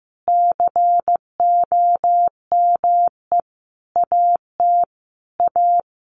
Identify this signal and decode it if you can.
Morse code, heard as a keyed tone, about 15 wpm, 700 Hz. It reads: COME AT A
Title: Sentences from Top 900 Words 15wpm